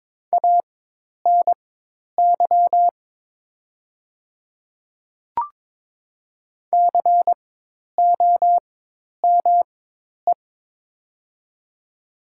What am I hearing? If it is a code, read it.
Morse code, heard as a keyed tone, about 22 wpm, spaced as for 10 wpm, 700 Hz. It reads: ANY COME
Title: 4x Character Spacing - Top 300 Words - 22wpm